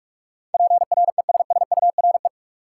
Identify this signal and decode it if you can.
Morse code, heard as a keyed tone, about 46 wpm, 700 Hz. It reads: PRESSURE
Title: Sentences from Top 1000 Words - Review Part 6 - New Word Drill - 45wpm